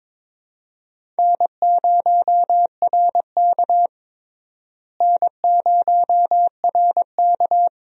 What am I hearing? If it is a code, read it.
Morse code, heard as a keyed tone, about 21 wpm, 700 Hz. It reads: N0RK N0RK
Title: USA Call Signs - 4 Characters - Repeated 2x - 22wpm